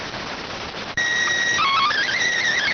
SSTV (slow-scan television), PD 290.